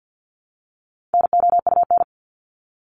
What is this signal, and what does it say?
Morse code, heard as a keyed tone, about 50 wpm, 700 Hz. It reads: NOUN
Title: Top 500 Words Repeated 50wpm